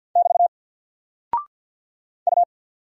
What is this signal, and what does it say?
Morse code, heard as a keyed tone, about 50 wpm, 700 Hz. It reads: <BT> U
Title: Single Character (Letter:Number:Punctuation:Prosign) 50wpm